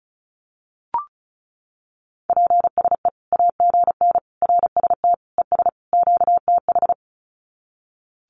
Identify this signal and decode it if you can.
Morse code, heard as a keyed tone, about 34 wpm, 700 Hz. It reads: PSE AGN RST ES QTH
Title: Common QSO Phrases 35wpm